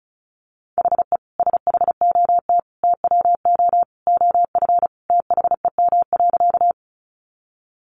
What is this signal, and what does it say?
Morse code, heard as a keyed tone, about 34 wpm, 700 Hz. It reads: HE SHOT TWO OF THEM.
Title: Sentences from Top 1100 Words 35wpm